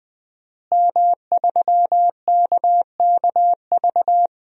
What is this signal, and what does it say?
Morse code, heard as a keyed tone, about 21 wpm, 700 Hz. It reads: M3KKV
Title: DXCC-100 Call Signs - 5 Characters - Repeated 3x - 20wpm